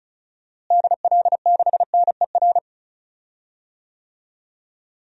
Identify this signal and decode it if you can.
Morse code, heard as a keyed tone, about 34 wpm, 700 Hz. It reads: DL6NER
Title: DXCC-100 Call Signs - 6 Characters - Repeated 3x - 35wpm